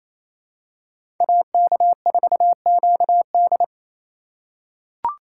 Morse code, heard as a keyed tone, about 28 wpm, 700 Hz. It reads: AK4QD
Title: USA Call Signs - 5 Characters - 4x Character Spacing - 28wpm